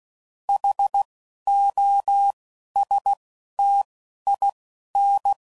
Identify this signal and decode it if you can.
Morse code, heard as a keyed tone, about 16 wpm, spaced as for 11 wpm, 800 Hz. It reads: HOSTIN
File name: Suffixes / Special INGs / Single INGs